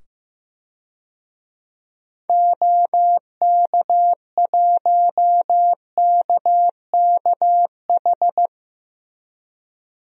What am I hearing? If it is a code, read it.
Morse code, heard as a keyed tone, about 15 wpm, 700 Hz. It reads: OK1KKH
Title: DXCC-100 Call Signs - 6 Characters - 15wpm